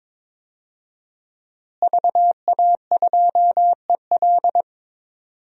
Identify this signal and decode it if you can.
Morse code, heard as a keyed tone, about 22 wpm, 700 Hz. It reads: VA2EL
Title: DXCC-100 Call Signs - 5 Characters - Repeated 2x - 22wpm